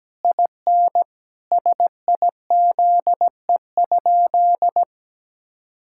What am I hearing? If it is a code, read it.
Morse code, heard as a keyed tone, about 17 wpm, 700 Hz. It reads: IN SIZE?
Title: Sentences from Top 1000 Words - Review Part 5 - 17wpm